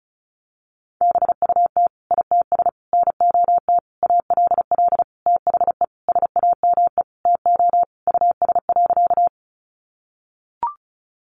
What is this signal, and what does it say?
Morse code, heard as a keyed tone, about 34 wpm, 700 Hz. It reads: BUT ITS NOT ALL THE SAME TO US.
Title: Sentences from Top 600 Words 35wpm